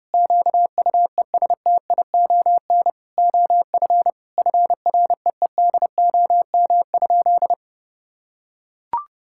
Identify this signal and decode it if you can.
Morse code, heard as a keyed tone, about 30 wpm, 700 Hz. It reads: QUESTION OF FREEDOM?
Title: Sentences from Top 1600 Words 30wpm